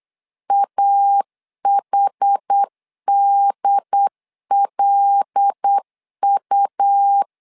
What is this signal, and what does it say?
Morse code, heard as a keyed tone, about 8 wpm, 800 Hz. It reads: AHDLU
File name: LEZIONI / 5 LBVC / 5 L 8wpm 2831